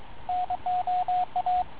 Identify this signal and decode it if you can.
Morse code, heard as a keyed tone, about 23 wpm, 720 Hz, with radio noise. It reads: NOA